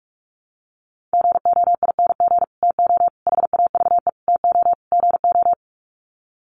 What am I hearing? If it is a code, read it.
Morse code, heard as a keyed tone, about 44 wpm, 700 Hz. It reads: GOING TO HAVE TO GO
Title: Sets of 5 Words Encore 45wpm